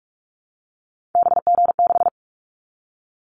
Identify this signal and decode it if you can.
Morse code, heard as a keyed tone, about 44 wpm, 700 Hz. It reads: BG6
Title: Callsign Prefixes for Top 100 DXCC - 45wpm